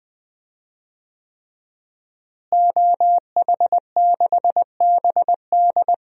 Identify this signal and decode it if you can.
Morse code, heard as a keyed tone, about 21 wpm, 700 Hz. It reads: OH6BD
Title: DXCC-100 Call Signs - 5 Characters - Repeated 3x - 20wpm